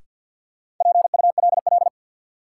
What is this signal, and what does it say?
Morse code, heard as a keyed tone, about 51 wpm, 700 Hz. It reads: PULL
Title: Sentences from Top 400 Words - New Word Drill - 50wpm